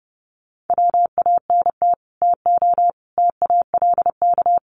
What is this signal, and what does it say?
Morse code, heard as a keyed tone, about 28 wpm, 700 Hz. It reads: WANT TO TALK
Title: Sets of 4 Words 30wpm